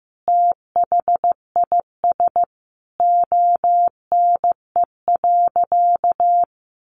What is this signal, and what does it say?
Morse code, heard as a keyed tone, about 15 wpm, 700 Hz. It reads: THIS ONE.
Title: Additional Easy Phrases 15wpm